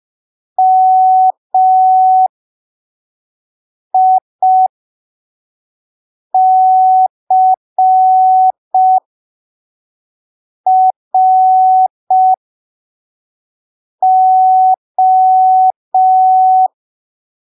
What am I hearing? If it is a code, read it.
Morse code, heard as a keyed tone, about 5 wpm, spaced as for 3 wpm, 750 Hz. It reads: MICRO
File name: arquivos / texto / texto 09 05ppm